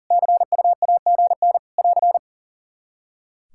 Morse code, heard as a keyed tone, about 40 wpm, 700 Hz. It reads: CU AGN <AR>
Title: Practice QSO - Exchange 4 - Ending - 40wpm